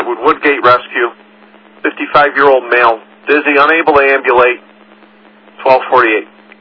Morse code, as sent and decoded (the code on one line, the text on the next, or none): none